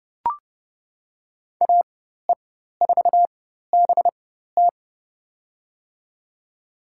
Morse code, heard as a keyed tone, about 31 wpm, spaced as for 14 wpm, 700 Hz. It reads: AE4BT
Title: USA Call Signs - 5 Characters - 4x Character Spacing - 30wpm